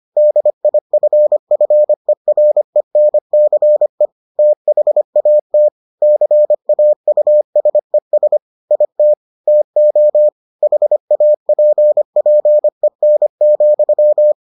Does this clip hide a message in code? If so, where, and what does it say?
Yes: Morse code from 0.2 to 14.3 s, heard as a keyed tone, about 25 wpm, 600 Hz. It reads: DIFFERENCE THAT CAUSES IT TO HAPPEN,